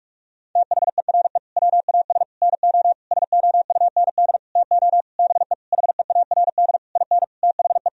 Morse code, heard as a keyed tone, about 41 wpm, 700 Hz. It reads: THERE WAS NO SOUND TO BE HEARD IN THE
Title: Sentences from Top 1000 Words - Review Part 3 - 45wpm